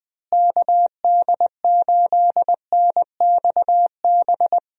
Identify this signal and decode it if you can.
Morse code, heard as a keyed tone, about 20 wpm, 700 Hz. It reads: KD8NXB
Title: USA Call Signs - 6 Characters - Repeated 3x - 20wpm